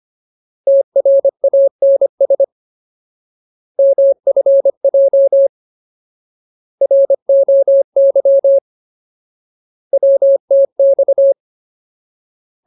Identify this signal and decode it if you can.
Morse code, heard as a keyed tone, about 25 wpm, 550 Hz. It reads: TRANS MFJ ROY WTX
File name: Ultimate Word List / 25WPM Ultimate